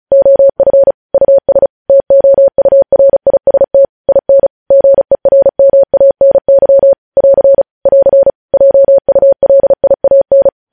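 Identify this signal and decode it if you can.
Morse code, heard as a keyed tone, about 36 wpm, 550 Hz. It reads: OF US TOURIST IN GERMANY <AR> <AR> JULIAN
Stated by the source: Morse news headlines practice at 35.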